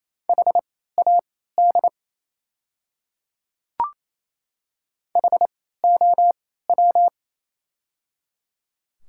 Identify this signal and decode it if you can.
Morse code, heard as a keyed tone, about 28 wpm, spaced as for 16 wpm, 690 Hz. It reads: HAD HOW
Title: 3x Character Spacing - Top 100 Words - 28wpm